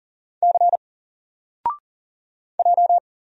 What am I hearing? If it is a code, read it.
Morse code, heard as a keyed tone, about 40 wpm, 700 Hz. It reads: C J
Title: Single Letter-Number Alt 40wpm